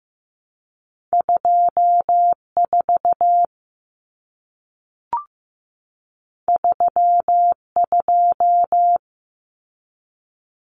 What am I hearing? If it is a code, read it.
Morse code, heard as a keyed tone, about 15 wpm, 700 Hz. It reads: 24 32
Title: Numbers - 2-Digit 15wpm